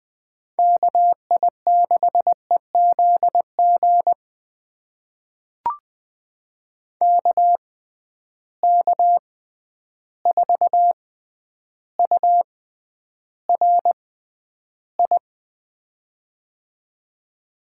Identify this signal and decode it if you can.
Morse code, heard as a keyed tone, about 20 wpm, spaced as for 10 wpm, 700 Hz. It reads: KI6EZG KK4URI
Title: USA Call Signs - 6 Characters - 6x Character Spacing - 20wpm